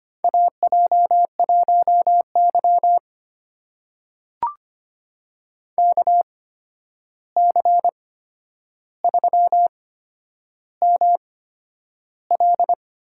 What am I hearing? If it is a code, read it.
Morse code, heard as a keyed tone, about 25 wpm, spaced as for 10 wpm, 700 Hz. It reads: AJ1Y KC3ML
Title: USA Call Signs - 8x Character Spacing - 25wpm